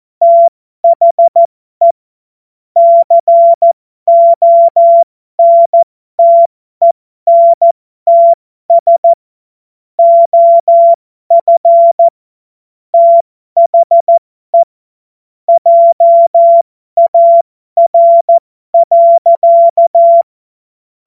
Morse code, heard as a keyed tone, about 14 wpm, spaced as for 12 wpm, 680 Hz. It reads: THE CONTENTS OF THE JAR.
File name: Morsekurs / Witze